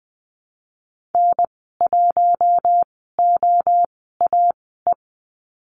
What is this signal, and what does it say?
Morse code, heard as a keyed tone, about 20 wpm, spaced as for 15 wpm, 700 Hz. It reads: N1OAE
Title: 147 Lesson 24g - Letter y - Call Signs 20wpm